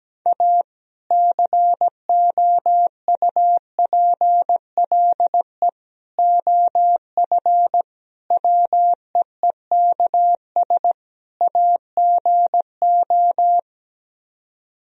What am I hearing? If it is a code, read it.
Morse code, heard as a keyed tone, about 17 wpm, 700 Hz. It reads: A COUPLE OF WEEKS AGO
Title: Speed-Racing - Sets of 5 Words 17wpm